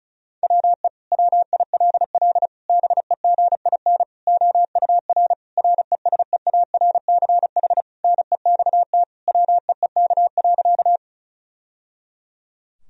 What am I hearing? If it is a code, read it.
Morse code, heard as a keyed tone, about 34 wpm, 700 Hz. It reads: WE WILL BEGIN OUR RESEARCH NEXT WEEK.
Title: Sentences from Top 1000 Words - Review Part 4 - 35wpm